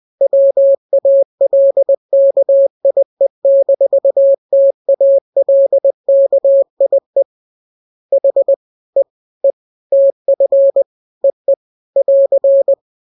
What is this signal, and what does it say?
Morse code, heard as a keyed tone, about 20 wpm, 550 Hz. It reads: WALKIE-TALKIE H E E TF EE <AR>